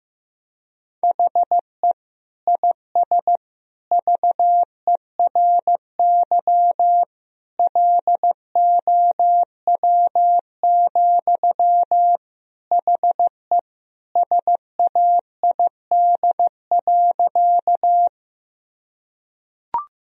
Morse code, heard as a keyed tone, about 16 wpm, 700 Hz. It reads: HE IS VERY LOW, HE SAID.
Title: Sentences from Top 1000 Words - Review Part 1 - 15wpm